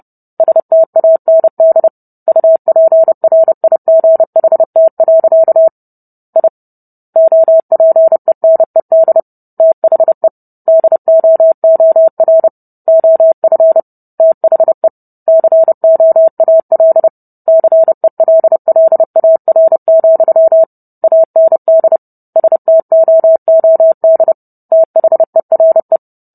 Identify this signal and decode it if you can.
Morse code, heard as a keyed tone, about 30 wpm, 650 Hz. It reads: STAND UPRIGHT. I OPENED THE DOOR OF THE COAL CELLAR, AND STOOD THERE